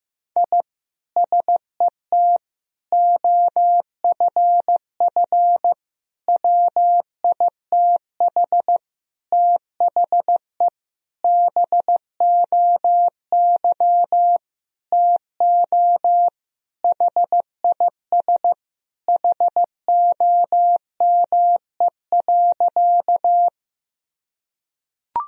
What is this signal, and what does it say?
Morse code, heard as a keyed tone, about 16 wpm, 700 Hz. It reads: I SET OFF WITH THE BOY TO HIS HOME.
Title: Sentences from Top 1000 Words - Review Part 1 - 15wpm